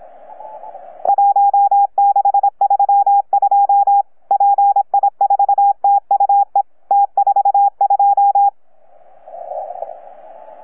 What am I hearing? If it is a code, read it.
Morse code, heard as a keyed tone, about 26 wpm, 810 Hz. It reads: 1632 PI4TUE T42
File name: UP2L-20-CW